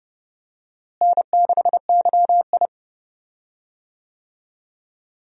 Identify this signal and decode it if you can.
Morse code, heard as a keyed tone, about 32 wpm, 700 Hz. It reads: N6YI
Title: USA Call Signs - 4 Characters - Repeated 2x - 30wpm